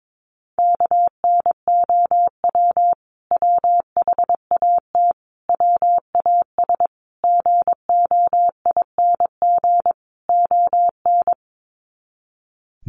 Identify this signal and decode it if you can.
Morse code, heard as a keyed tone, about 22 wpm, 700 Hz. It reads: KNOW WHAT WAS GOING ON
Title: Sets of 5 Words Encore2 22wpm